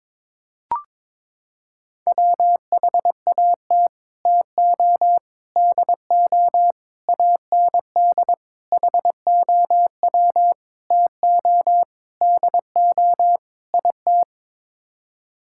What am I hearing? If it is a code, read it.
Morse code, heard as a keyed tone, about 22 wpm, 700 Hz. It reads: WHAT TO DO AND HOW TO DO IT
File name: practice / 22wpm / Sets of 8 Words - 22wpm